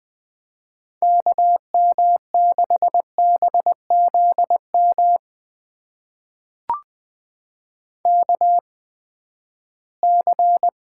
Morse code, heard as a keyed tone, about 20 wpm, spaced as for 14 wpm, 700 Hz. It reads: KM6BZM KC
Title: USA Call Signs - 6 Characters - 8x Character Spacing - 20wpm